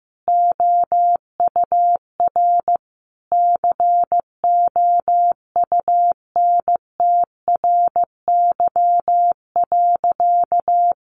Morse code, heard as a keyed tone, about 14 wpm, 700 Hz. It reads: OUR COUNTRY.
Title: Sentences from Top 1300 Words 15wpm